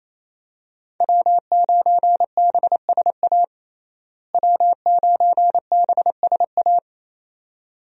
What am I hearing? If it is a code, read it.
Morse code, heard as a keyed tone, about 27 wpm, 700 Hz. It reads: W9BSA W9BSA
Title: USA Call Signs - 5 Characters - Repeated 2x - 28wpm